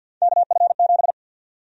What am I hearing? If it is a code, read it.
Morse code, heard as a keyed tone, about 50 wpm, 700 Hz. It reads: KF7
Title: USA Call Sign Prefixes - 50wpm